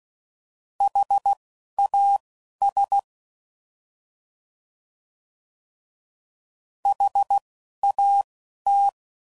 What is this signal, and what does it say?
Morse code, heard as a keyed tone, about 16 wpm, spaced as for 11 wpm, 800 Hz. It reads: HAS HAT